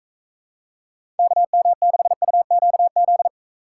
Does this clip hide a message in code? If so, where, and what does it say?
Yes: Morse code from 1.2 to 3.3 s, heard as a keyed tone, about 44 wpm, 700 Hz. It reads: KM6UQZ